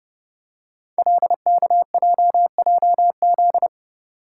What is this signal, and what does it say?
Morse code, heard as a keyed tone, about 30 wpm, 700 Hz. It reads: LKJJZ
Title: Code Groups - 5 Letters - 30wpm